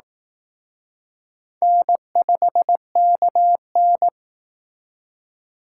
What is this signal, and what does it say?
Morse code, heard as a keyed tone, about 18 wpm, 700 Hz. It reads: N5KN
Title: Speed Racing - USA 1x2 Call Signs 15wpm